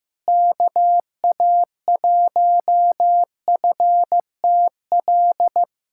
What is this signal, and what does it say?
Morse code, heard as a keyed tone, about 15 wpm, 700 Hz. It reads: KA1FTL
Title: USA Callsign DE Callsign - 6 Characters - 15wpm